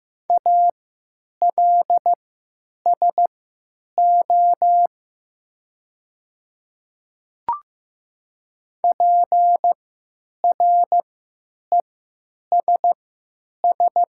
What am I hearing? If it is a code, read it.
Morse code, heard as a keyed tone, about 15 wpm, spaced as for 9 wpm, 700 Hz. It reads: ALSO PRESS
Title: 3x Character Spacing - Top 500 Words - 15wpm